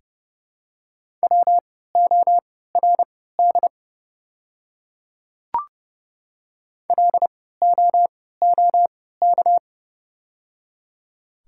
Morse code, heard as a keyed tone, about 29 wpm, spaced as for 17 wpm, 700 Hz. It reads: WORD LOOK
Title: 3x Character Spacing - Top 100 Words - 30wpm